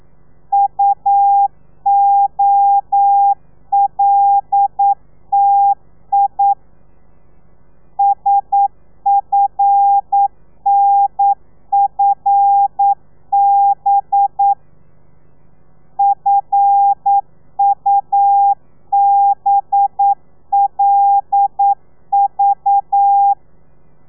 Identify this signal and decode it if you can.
Morse code, heard as a keyed tone, about 9 wpm, 800 Hz. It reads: UOLTI SFNFB FUBLV
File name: telegrafia / i2tfj / Lez5 40